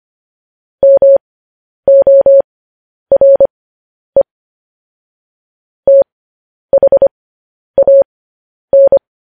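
Morse code, heard as a keyed tone, about 25 wpm, spaced as for 10 wpm, 550 Hz. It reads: MORE THAN
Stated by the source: Morse code transmission of news headlines.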